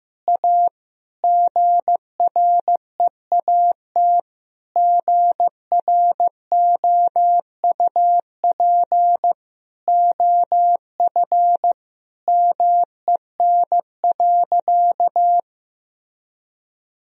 Morse code, heard as a keyed tone, about 15 wpm, 700 Hz. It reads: A GREAT GROUP OF MEN.
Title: Sentences from Top 1000 Words - Review Part 1 - 15wpm